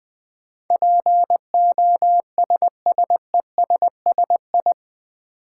Morse code, heard as a keyed tone, about 20 wpm, 700 Hz. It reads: POSSESSI
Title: Build A Word - 4 Letters At A Time - 10-Letter Words - 20wpm